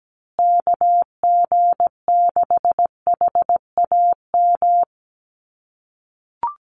Morse code, heard as a keyed tone, about 17 wpm, 700 Hz. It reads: KG6HAM
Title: USA Call Signs Repeated 2x 17wpm